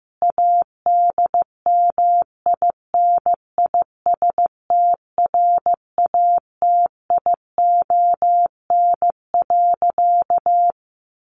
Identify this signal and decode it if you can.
Morse code, heard as a keyed tone, about 15 wpm, 700 Hz. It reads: ADMINISTRATION.
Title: Sentences from Top 1300 Words 15wpm